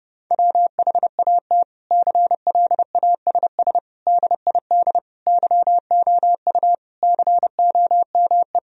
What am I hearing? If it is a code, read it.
Morse code, heard as a keyed tone, about 32 wpm, 700 Hz. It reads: WHAT CLASS DID YOU COME
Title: Sentences from Top 1000 Words - Review Part 2 - 30wpm